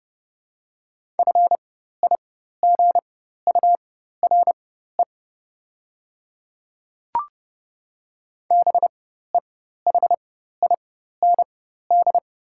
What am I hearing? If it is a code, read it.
Morse code, heard as a keyed tone, about 30 wpm, spaced as for 14 wpm, 700 Hz. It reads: FIGURE BEHIND
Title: 4x Character Spacing - Top 500 Words - 30wpm